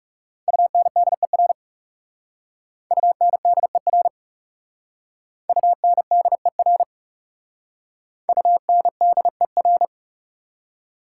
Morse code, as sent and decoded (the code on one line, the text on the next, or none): ..- -. -.. .-. / ..- -. -.. .-. / ..- -. -.. .-. / ..- -. -.. .-.
UNDER UNDER UNDER UNDER